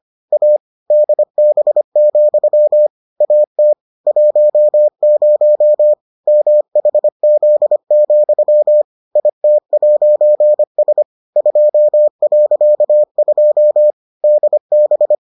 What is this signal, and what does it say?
Morse code, heard as a keyed tone, about 25 wpm, 600 Hz. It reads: A DB, AT 10 MHZ, IT'S 2.2 DB